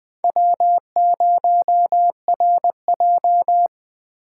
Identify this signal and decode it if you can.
Morse code, heard as a keyed tone, about 23 wpm, 700 Hz. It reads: W0RJ
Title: Speed Racing - USA 1x2 Call Signs 20wpm